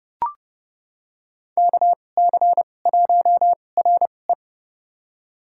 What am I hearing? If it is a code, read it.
Morse code, heard as a keyed tone, about 30 wpm, spaced as for 22 wpm, 700 Hz. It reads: KC1RE
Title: USA Call Signs - 5 Characters - 2x Character Spacing - 30wpm